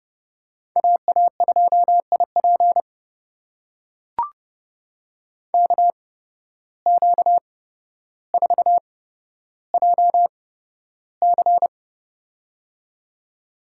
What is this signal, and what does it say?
Morse code, heard as a keyed tone, about 30 wpm, spaced as for 13 wpm, 700 Hz. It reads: AA2IP KQ4JC
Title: USA Call Signs - 5 Characters - 8x Character Spacing - 30wpm